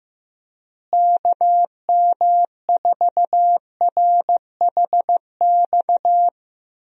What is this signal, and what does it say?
Morse code, heard as a keyed tone, about 15 wpm, 700 Hz. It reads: KM4RHX
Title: USA Call Signs - 6 Characters - Repeated 3x - 15wpm